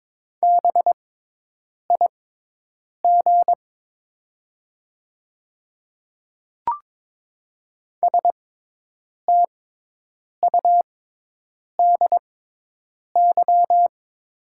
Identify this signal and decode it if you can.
Morse code, heard as a keyed tone, about 22 wpm, spaced as for 8 wpm, 700 Hz. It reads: BIG STUDY
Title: 6x Character Spacing - Top 300 Words - 22wpm